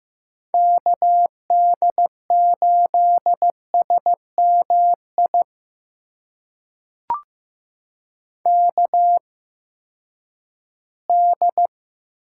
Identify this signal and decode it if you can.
Morse code, heard as a keyed tone, about 15 wpm, spaced as for 10 wpm, 700 Hz. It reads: KD8SMI KD
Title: USA Call Signs - 6 Characters - 8x Character Spacing - 15wpm